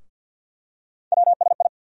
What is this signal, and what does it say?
Morse code, heard as a keyed tone, about 48 wpm, 700 Hz. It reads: WSI
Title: USA Call Sign Postfixes - 3 Letters 50wpm